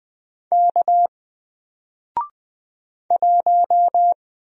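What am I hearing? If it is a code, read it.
Morse code, heard as a keyed tone, about 20 wpm, 700 Hz. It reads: K 1
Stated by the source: Single letter and number Morse drill at 20 wpm, from N4LCD.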